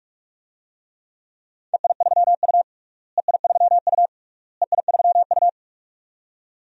Morse code, heard as a keyed tone, about 48 wpm, 700 Hz. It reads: EI3U EI3U EI3U
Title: DXCC-100 Call Signs - 4 Characters - Repeated 3x - 45wpm